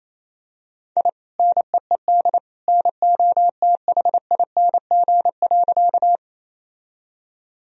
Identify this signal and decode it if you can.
Morse code, heard as a keyed tone, about 28 wpm, 700 Hz. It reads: I NEED NOTHING.
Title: Sentences from Top 1000 Words - Review Part 1 - 28wpm